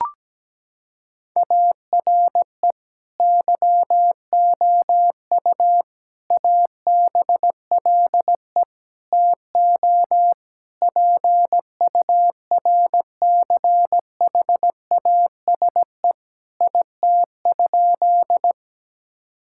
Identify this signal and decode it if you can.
Morse code, heard as a keyed tone, about 17 wpm, 700 Hz. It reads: ARE YOU ABLE TO PURCHASE IT?
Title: Sentences from Top 1000 Words - Review Part 5 - 17wpm